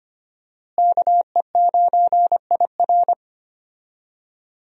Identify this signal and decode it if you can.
Morse code, heard as a keyed tone, about 25 wpm, 700 Hz. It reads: KE9IR
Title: USA Call Signs - 5 Characters - Repeated 2x - 25wpm